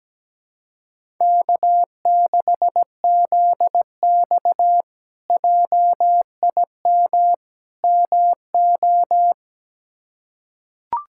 Morse code, heard as a keyed tone, about 17 wpm, 700 Hz. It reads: K6ZX JIM MO
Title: K1USN SST QSO Practice - 17wpm